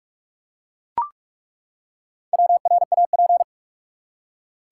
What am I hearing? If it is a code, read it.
Morse code, heard as a keyed tone, about 42 wpm, 700 Hz. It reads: WRAP
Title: Four Letter Words - 45wpm